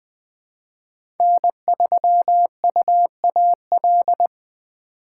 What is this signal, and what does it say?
Morse code, heard as a keyed tone, about 20 wpm, 700 Hz. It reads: N3UAL
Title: USA Call Signs - 5 Characters - 4x Character Spacing - 20wpm